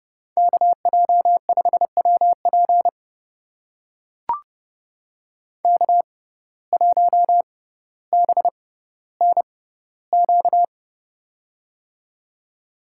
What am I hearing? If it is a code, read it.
Morse code, heard as a keyed tone, about 30 wpm, spaced as for 15 wpm, 700 Hz. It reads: KJ5WP K1BNQ
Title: USA Call Signs - 5 Characters - 6x Character Spacing - 30wpm